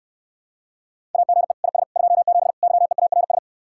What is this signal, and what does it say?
Morse code, heard as a keyed tone, about 68 wpm, 700 Hz. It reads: AGE IS 27 YEARS